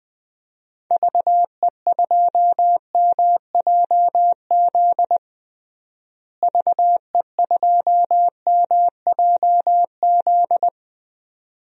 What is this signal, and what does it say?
Morse code, heard as a keyed tone, about 20 wpm, 700 Hz. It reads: VE2MJZ VE2MJZ
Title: DXCC-100 Call Signs - 6 Characters - Repeated 2x - 20wpm